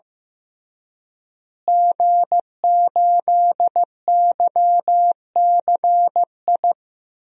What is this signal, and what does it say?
Morse code, heard as a keyed tone, about 15 wpm, 700 Hz. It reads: G8YCI
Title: DXCC-100 Call Signs - 5 Characters - Repeated 2x - 15wpm